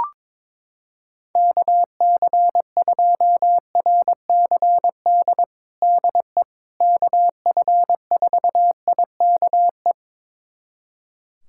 Morse code, heard as a keyed tone, about 22 wpm, 700 Hz. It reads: KC2RCD DE KF4IKE
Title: USA Callsign DE Callsign 22wpm